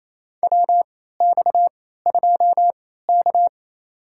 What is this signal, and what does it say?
Morse code, heard as a keyed tone, about 28 wpm, spaced as for 16 wpm, 700 Hz. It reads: WX2K
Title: USA Call Signs - 4 Characters - 3x Character Spacing - 28wpm